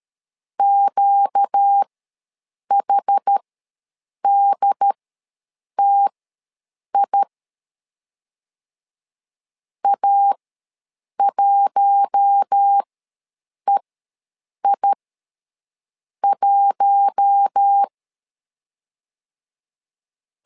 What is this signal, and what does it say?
Morse code, heard as a keyed tone, about 13 wpm, spaced as for 7 wpm, 800 Hz. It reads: QHDTI A1EI1
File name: LEZIONI / 4 Q1 ws 12wpm 2129